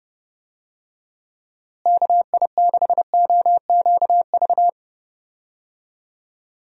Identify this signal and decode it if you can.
Morse code, heard as a keyed tone, about 31 wpm, 700 Hz. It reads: KI6OQV
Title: USA Call Signs - 6 Characters - Repeated 2x - 30wpm